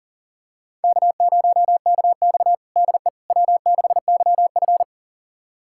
Morse code, heard as a keyed tone, about 38 wpm, 700 Hz. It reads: K0KX DE W6YF
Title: USA Callsign DE Callsign - 4 Characters - 40wpm